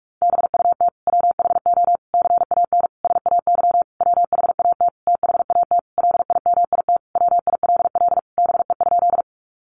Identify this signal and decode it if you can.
Morse code, heard as a keyed tone, about 44 wpm, 700 Hz. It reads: BUT WHO CAN SAY WHAT THAT LIMIT WILL BE?
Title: Sentences from Top 1500 Words 45wpm